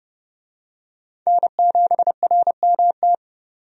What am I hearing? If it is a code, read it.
Morse code, heard as a keyed tone, about 31 wpm, 700 Hz. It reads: N7RMT
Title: USA Call Signs - 5 Characters - 30wpm